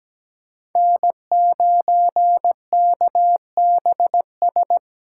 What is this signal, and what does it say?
Morse code, heard as a keyed tone, about 17 wpm, 700 Hz. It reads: N9KBS
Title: USA Call Signs - 5 Characters - 4x Character Spacing - 17wpm